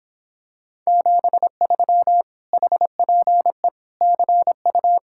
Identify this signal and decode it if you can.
Morse code, heard as a keyed tone, about 26 wpm, 700 Hz. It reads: 73 HPE CU